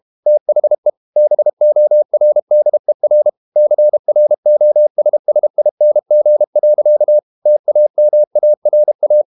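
Morse code, heard as a keyed tone, about 32 wpm, 600 Hz. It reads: THE BORDER CROSSING. TAMARA
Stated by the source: Morse code practice at 32 wpm from CW med Gnister, reading Ken Follett's novel Never in English.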